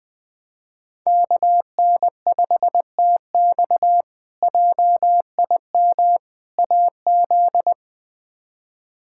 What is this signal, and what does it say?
Morse code, heard as a keyed tone, about 21 wpm, 700 Hz. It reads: KN5TX JIM AZ
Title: K1USN SST QSO Practice - 20wpm